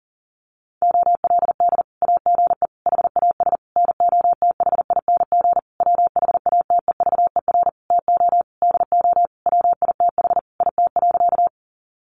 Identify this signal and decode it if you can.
Morse code, heard as a keyed tone, about 40 wpm, 700 Hz. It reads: OLD AGE HAS NOTHING WHATEVER TO DO WITH IT.
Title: Sentences from Top 1100 Words 40wpm